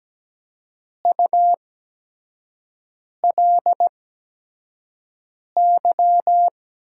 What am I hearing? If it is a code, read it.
Morse code, heard as a keyed tone, about 17 wpm, spaced as for 5 wpm, 700 Hz. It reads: ULY
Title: USA Call Signs - 5 Characters - 8x Character Spacing - 17wpm